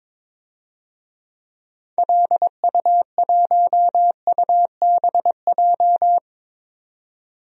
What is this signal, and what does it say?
Morse code, heard as a keyed tone, about 21 wpm, 700 Hz. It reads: LU1UBJ
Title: DXCC-100 Call Signs - 6 Characters - 22wpm